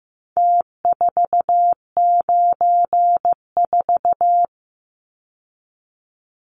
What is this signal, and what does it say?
Morse code, heard as a keyed tone, about 15 wpm, 700 Hz. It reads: T494
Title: Numbers - Contest Style - 4-Digit 15wpm